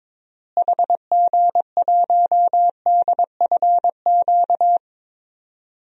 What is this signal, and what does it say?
Morse code, heard as a keyed tone, about 22 wpm, 700 Hz. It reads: HG1DFQ
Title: DXCC-100 Call Signs - 6 Characters - Repeated 3x - 22wpm